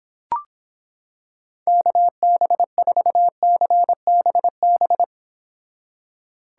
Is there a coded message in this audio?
Yes: Morse code from 1.7 to 5.0 s, heard as a keyed tone, about 26 wpm, 700 Hz. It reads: KB4CBB